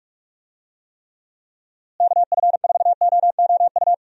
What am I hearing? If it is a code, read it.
Morse code, heard as a keyed tone, about 46 wpm, 700 Hz. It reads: KF4OOU
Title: USA Call Signs - 6 Characters - Repeated 2x - 45wpm